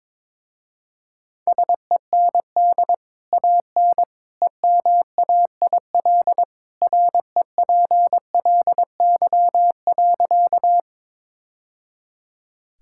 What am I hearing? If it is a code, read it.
Morse code, heard as a keyed tone, about 22 wpm, 700 Hz. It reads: SEND AN EMAIL REPLY.
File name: practice / 22wpm / Easy Sentences with Hints - 22wpm